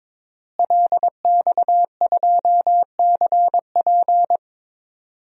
Morse code, heard as a keyed tone, about 22 wpm, 700 Hz. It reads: LX2CP
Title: DXCC-100 Call Signs - 5 Characters - Repeated 3x - 22wpm